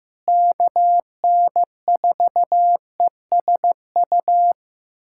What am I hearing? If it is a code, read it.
Morse code, heard as a keyed tone, about 16 wpm, 700 Hz. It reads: KN4ESU
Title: USA Callsign DE Callsign 15wpm